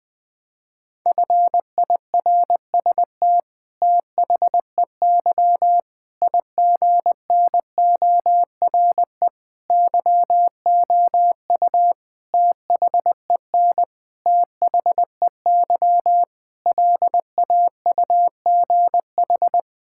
Morse code, heard as a keyed tone, about 20 wpm, 700 Hz. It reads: FIRST THEY IGNORE YOU THEN THEY LAUGH
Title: Sentences with 8-Word Phrases - 20wpm